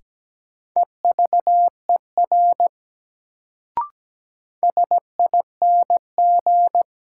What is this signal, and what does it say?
Morse code, heard as a keyed tone, about 17 wpm, 700 Hz. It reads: EVER SING
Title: Sentences from Top 400 Words - New Word Drill - 17wpm